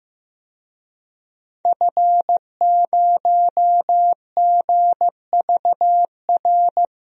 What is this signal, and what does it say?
Morse code, heard as a keyed tone, about 15 wpm, 700 Hz. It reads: F0GVR
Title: DXCC-100 Call Signs - 5 Characters - Repeated 3x - 15wpm